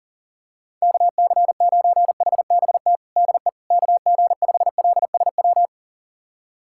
Morse code, heard as a keyed tone, about 40 wpm, 700 Hz. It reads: KC9HBT DE KG5LSW
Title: USA Callsign DE Callsign 40wpm